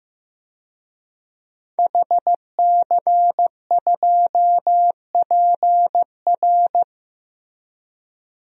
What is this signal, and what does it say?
Morse code, heard as a keyed tone, about 15 wpm, 700 Hz. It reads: HC2PR
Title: DXCC-100 Call Signs - 5 Characters - 15wpm